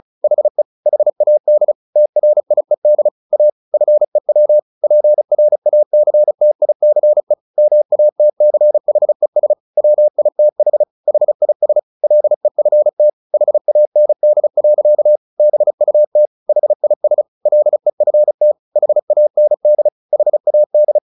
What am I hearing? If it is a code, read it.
Morse code, heard as a keyed tone, about 34 wpm, 600 Hz. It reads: HE HAD TRIED A FEW PRACTICE MATCHES WITH HIS LEFT HAND. BUT HIS LEFT HAND HAD
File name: the old man and the sea 2 35wpm